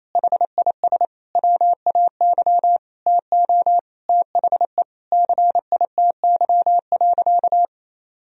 Morse code, heard as a keyed tone, about 27 wpm, 700 Hz. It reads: HIS WAY TO THE CITY.
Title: Sentences from Top 1000 Words - Review Part 1 - 28wpm